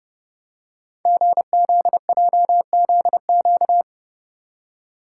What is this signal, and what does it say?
Morse code, heard as a keyed tone, about 30 wpm, 690 Hz. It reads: GZJZQ
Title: Code Groups - 5 Letter:Numbers - 30wpm